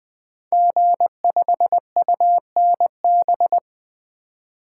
Morse code, heard as a keyed tone, about 20 wpm, 700 Hz. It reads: G5UNB
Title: Code Groups - 5 Letter:Numbers - 20wpm